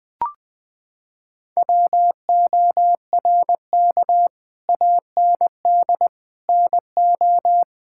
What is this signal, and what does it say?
Morse code, heard as a keyed tone, about 20 wpm, 700 Hz. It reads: WORK AND NO
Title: Sentences with 8-Word Phrases - 20wpm